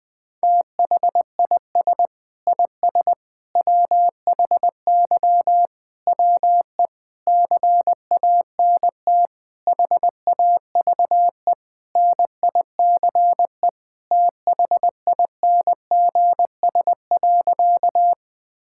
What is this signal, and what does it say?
Morse code, heard as a keyed tone, about 21 wpm, 700 Hz. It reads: THIS IS WHY WE CANT HAVE NICE THINGS.
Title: Sentences with 8-Word Phrases - 20wpm